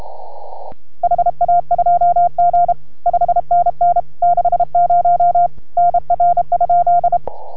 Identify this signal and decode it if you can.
Morse code, heard as a keyed tone, about 31 wpm, 690 Hz. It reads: HA2G 5NN 60 NR?